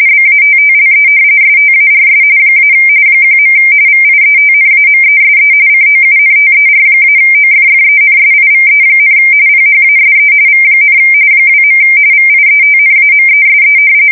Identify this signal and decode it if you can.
RTTY, about 45.45 baud. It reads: BATTLE STARS - 1. VIETNAMESE COUNTER-OFFENSIVE, PHASE V, 2. VIETNAMESE
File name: fsk170 / battleshipnj